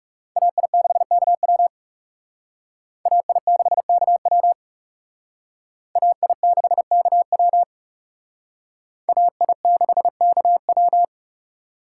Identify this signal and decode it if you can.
Morse code, heard as a keyed tone, about 38 wpm, 700 Hz. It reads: AI6KW AI6KW AI6KW AI6KW